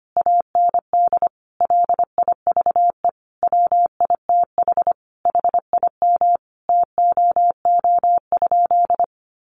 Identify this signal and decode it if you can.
Morse code, heard as a keyed tone, about 25 wpm, 700 Hz. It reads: AND LIVE WITH HIM TOO?
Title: Sentences from Top 500 Words 25wpm